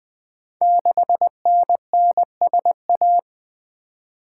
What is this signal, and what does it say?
Morse code, heard as a keyed tone, about 20 wpm, 700 Hz. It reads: 6NNSA
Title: Code Groups - 5 Letter:Numbers - 20wpm